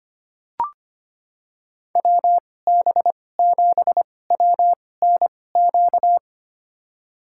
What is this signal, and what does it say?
Morse code, heard as a keyed tone, about 25 wpm, spaced as for 18 wpm, 700 Hz. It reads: WB7WNQ
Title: USA Call Signs - 6 Characters - 2x Character Spacing - 25wpm